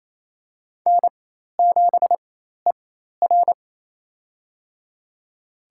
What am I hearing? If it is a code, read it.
Morse code, heard as a keyed tone, about 28 wpm, spaced as for 13 wpm, 700 Hz. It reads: N7ER
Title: USA Call Signs - 4 Characters - 4x Character Spacing - 28wpm